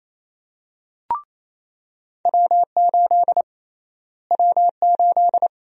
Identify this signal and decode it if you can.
Morse code, heard as a keyed tone, about 28 wpm, 700 Hz. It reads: W8 W8
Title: USA Call Sign Prefixes - Repeated 2x - 28wpm